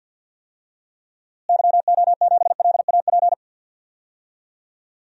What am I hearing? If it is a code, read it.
Morse code, heard as a keyed tone, about 52 wpm, 700 Hz. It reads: YO7LAP
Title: DXCC-100 Call Signs - 6 Characters - Repeated 3x - 50wpm